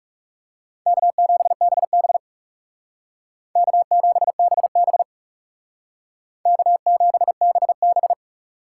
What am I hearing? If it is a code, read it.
Morse code, heard as a keyed tone, about 36 wpm, 700 Hz. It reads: K7BB K7BB K7BB
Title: Speed Racing - USA 1x2 Call Signs 30wpm